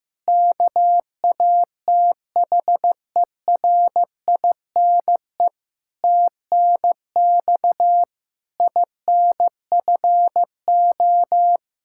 Morse code, heard as a keyed tone, about 16 wpm, 700 Hz. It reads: KATHERINE TNX INFO
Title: Practice QSO - Exchange 3 - More Info - 15wpm